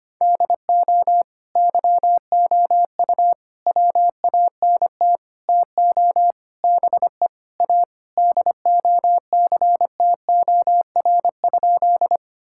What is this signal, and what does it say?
Morse code, heard as a keyed tone, about 25 wpm, 700 Hz. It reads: DO YOU WANT TO BE A DOCTOR?
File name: practice / 25wpm / Sentences with 4-Word Phrases - 25wpm